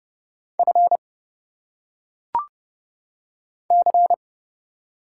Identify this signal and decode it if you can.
Morse code, heard as a keyed tone, about 31 wpm, 700 Hz. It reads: F C
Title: Single Letters - Phonetic Pronunciation 30wpm